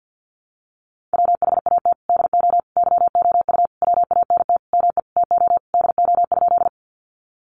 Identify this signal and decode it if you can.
Morse code, heard as a keyed tone, about 50 wpm, 700 Hz. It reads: WHAT DO YOU WANT ME TO DO?
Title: Sentences from Top 200 Words 50wpm